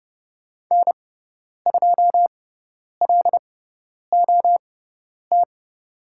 Morse code, heard as a keyed tone, about 30 wpm, spaced as for 10 wpm, 700 Hz. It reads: N2LOT
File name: Lesson 18g - Letter m - Call Signs 30wpm